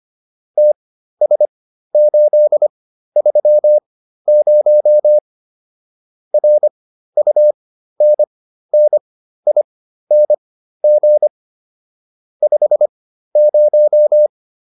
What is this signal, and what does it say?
Morse code, heard as a keyed tone, about 25 wpm, spaced as for 13 wpm, 600 Hz. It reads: TS830 RUNNING 50